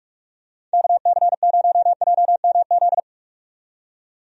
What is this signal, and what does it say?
Morse code, heard as a keyed tone, about 46 wpm, 700 Hz. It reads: KC0JMZ
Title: USA Call Signs - 6 Characters - 45wpm